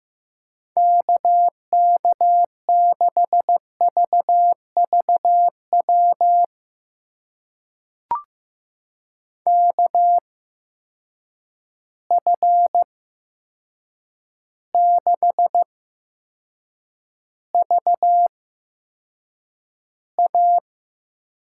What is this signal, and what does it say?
Morse code, heard as a keyed tone, about 15 wpm, spaced as for 7 wpm, 700 Hz. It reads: KK6VVW KF6VA
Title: USA Call Signs - 6 Characters - 8x Character Spacing - 15wpm